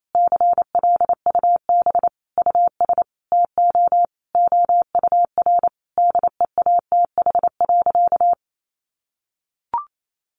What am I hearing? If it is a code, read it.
Morse code, heard as a keyed tone, about 28 wpm, 700 Hz. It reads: CLUB US TO OUR DEATH.
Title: Sentences from Top 1200 Words 28wpm